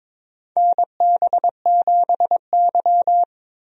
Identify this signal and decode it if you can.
Morse code, heard as a keyed tone, about 21 wpm, 700 Hz. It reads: NB7Y
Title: USA Call Signs - 4 Characters - Repeated 2x - 22wpm